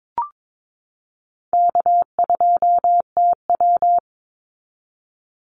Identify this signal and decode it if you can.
Morse code, heard as a keyed tone, about 22 wpm, 700 Hz. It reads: K2TW
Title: Easy Call Signs Repeated 2x - 22wpm